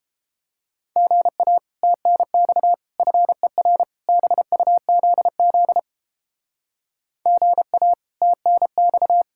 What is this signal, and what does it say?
Morse code, heard as a keyed tone, about 29 wpm, 700 Hz. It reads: GA TNX FER BUZZ GA TNX